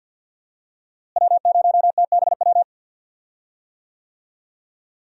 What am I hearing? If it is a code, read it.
Morse code, heard as a keyed tone, about 47 wpm, 700 Hz. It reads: W0TBW
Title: USA Call Signs - 5 Characters - Repeated 3x - 50wpm